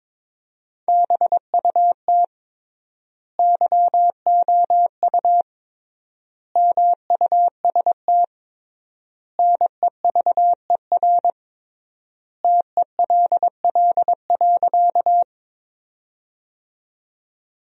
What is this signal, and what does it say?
Morse code, heard as a keyed tone, about 22 wpm, 700 Hz. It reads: BUT YOU MUST NEVER TELL.
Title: 3x Word-Spacing - Sentences from Top 300 Words - 22wpm